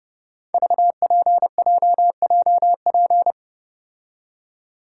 Morse code, heard as a keyed tone, about 28 wpm, 700 Hz. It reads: VPJJP
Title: Code Groups - 5 Letters - 30wpm